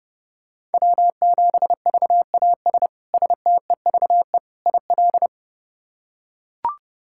Morse code, heard as a keyed tone, about 32 wpm, 700 Hz. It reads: W7VAS STEVE IL
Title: K1USN SST QSO Practice - 30wpm